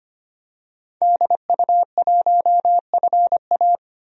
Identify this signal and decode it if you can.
Morse code, heard as a keyed tone, about 25 wpm, 700 Hz. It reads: DU1FA
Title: DXCC-100 Call Signs - 5 Characters - 25wpm